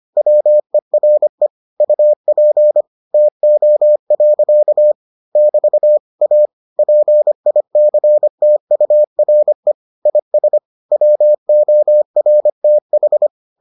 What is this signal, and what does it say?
Morse code, heard as a keyed tone, about 25 wpm, 600 Hz. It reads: WERE UP TO. <BT> A PICTURE IS WORTH